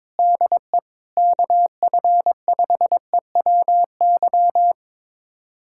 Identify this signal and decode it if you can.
Morse code, heard as a keyed tone, about 22 wpm, 700 Hz. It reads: DE KF5EWY
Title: USA Callsign DE Callsign - 6 Characters - 22wpm